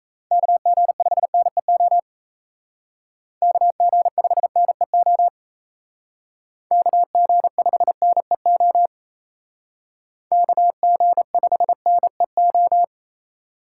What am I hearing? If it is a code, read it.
Morse code, heard as a keyed tone, about 32 wpm, 700 Hz. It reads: KG5NEO KG5NEO KG5NEO KG5NEO
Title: Speed Racing - USA 2x3 Call Signs 28wpm